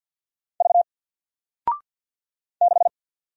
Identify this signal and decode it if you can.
Morse code, heard as a keyed tone, about 53 wpm, 700 Hz. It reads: V 6
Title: Single Letter-Number 50wpm